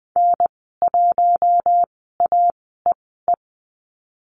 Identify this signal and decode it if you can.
Morse code, heard as a keyed tone, about 19 wpm, spaced as for 14 wpm, 700 Hz. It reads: N1AEE
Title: 048 Lesson 9g - Number 4 - Call Signs 20wpm